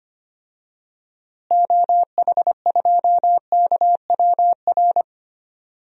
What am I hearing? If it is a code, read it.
Morse code, heard as a keyed tone, about 25 wpm, 700 Hz. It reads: OH2KWR
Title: DXCC-100 Call Signs - 6 Characters - Repeated 2x - 25wpm